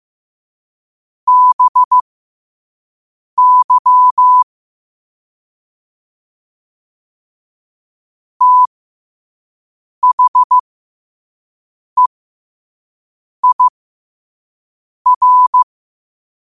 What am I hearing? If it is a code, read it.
Morse code, heard as a keyed tone, about 15 wpm, spaced as for 5 wpm, 1000 Hz. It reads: BY THEIR
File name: qotdfiles / 5wpm